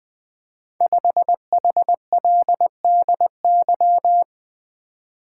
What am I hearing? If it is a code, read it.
Morse code, heard as a keyed tone, about 20 wpm, 700 Hz. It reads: 5HLDY
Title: Code Groups - 5 Letter:Numbers - 20wpm